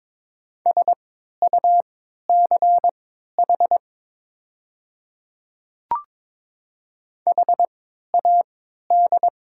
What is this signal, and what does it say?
Morse code, heard as a keyed tone, about 22 wpm, spaced as for 13 wpm, 700 Hz. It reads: SUCH HAD
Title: 3x Character Spacing - Top 200 Words - 22wpm